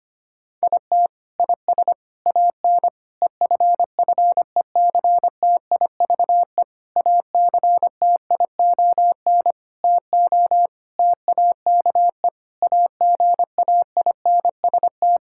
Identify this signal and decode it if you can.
Morse code, heard as a keyed tone, about 25 wpm, 700 Hz. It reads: IT IS AN EFFECTIVE ACTION TO TAKE AGAINST
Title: Sentences from Top 1000 Words - Review Part 6 - 25wpm